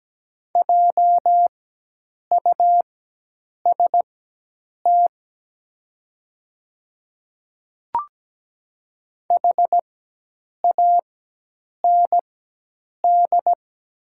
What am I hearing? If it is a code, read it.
Morse code, heard as a keyed tone, about 17 wpm, spaced as for 8 wpm, 700 Hz. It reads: JUST HAND
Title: 4x Character Spacing - Top 200 Words - 17wpm